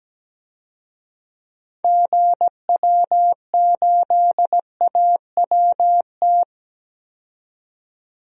Morse code, heard as a keyed tone, about 17 wpm, 700 Hz. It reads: GW8AWT
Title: DXCC-100 Call Signs - 6 Characters - 17wpm